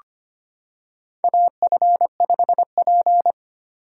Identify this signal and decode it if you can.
Morse code, heard as a keyed tone, about 25 wpm, 700 Hz. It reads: AF5P
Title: K1USN SST QSO Practice - 25wpm